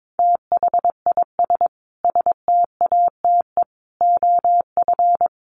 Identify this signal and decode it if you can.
Morse code, heard as a keyed tone, about 22 wpm, 700 Hz. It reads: THIS STATE OF
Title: Sentences from Top 1100 Words 22wpm